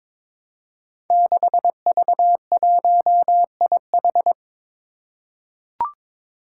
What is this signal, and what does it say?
Morse code, heard as a keyed tone, about 23 wpm, 700 Hz. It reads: 6V1IH
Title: Five Challenging Letter-Number Combinations 22wpm